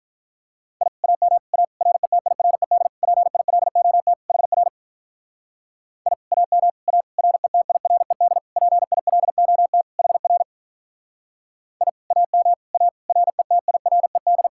Morse code, heard as a keyed tone, about 46 wpm, 700 Hz. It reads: I AM A RETIRED PILOT HR I AM A RETIRED PILOT HR I AM A RETIRED